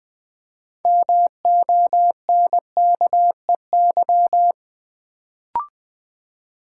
Morse code, heard as a keyed tone, about 20 wpm, 700 Hz. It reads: MONKEY
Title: Build A Word - 2 Letters At A Time - 6-Letter Words - 20wpm